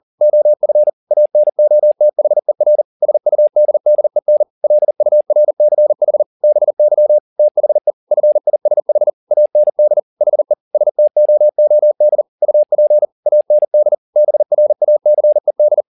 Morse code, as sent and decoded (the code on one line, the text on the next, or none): --- ..-. / .- -. --- - .... .-. / ... ..- -.. -.. -. / .-.. ..- .-. -.-. .... / -... -.-- / - .... / ..-. .. ... .... / .- -. -.. / .... / ... - --- --- -.. / ..- .--. / .- -. -.. / -... .-. .- -.-. -..
OF ANOTHER SUDDEN LURCH BY THE FISH AND HE STOOD UP AND BRACED